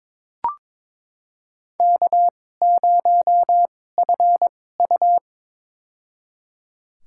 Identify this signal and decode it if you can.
Morse code, heard as a keyed tone, about 22 wpm, spaced as for 16 wpm, 700 Hz. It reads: K0FU
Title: USA Call Signs - 4 Characters - 2x Character Spacing - 22wpm